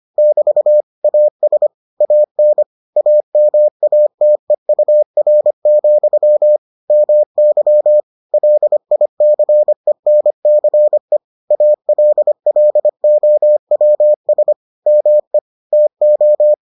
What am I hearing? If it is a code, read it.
Morse code, heard as a keyed tone, about 25 wpm, 600 Hz. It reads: <BT> AS AN AMATEUR, MY LICENCE ALLOWS ME TO